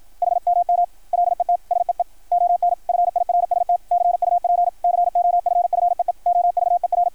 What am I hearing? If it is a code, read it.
Morse code, heard as a keyed tone, about 53 wpm, 660 Hz. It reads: HMM LET SEE ON FIRST CUP COFFEE OVER
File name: archived / CRSnet